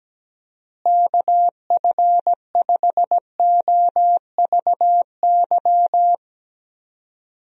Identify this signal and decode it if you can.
Morse code, heard as a keyed tone, about 17 wpm, 700 Hz. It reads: KF5OVY
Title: USA Call Signs - 6 Characters - Repeated 2x - 17wpm